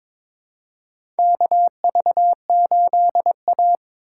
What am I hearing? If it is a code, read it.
Morse code, heard as a keyed tone, about 22 wpm, 700 Hz. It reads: KV8A
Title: USA Call Signs - 4 Characters - 2x Character Spacing - 22wpm